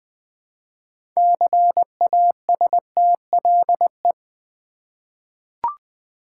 Morse code, heard as a keyed tone, about 20 wpm, 700 Hz. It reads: CASTLE
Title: 6 Letter Words - First 3 Letters Given - 20wpm